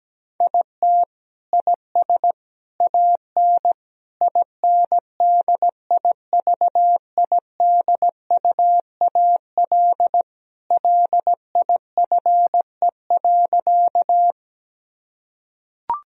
Morse code, heard as a keyed tone, about 17 wpm, 700 Hz. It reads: IT IS AN INDIVIDUAL LIFE.
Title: Sentences from Top 1000 Words - Review Part 4 - 17wpm